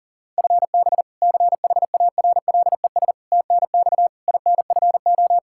Morse code, heard as a keyed tone, about 37 wpm, 700 Hz. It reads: FB CHARLES TNX INFO
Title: Practice QSO - Exchange 3 - More Info - 40wpm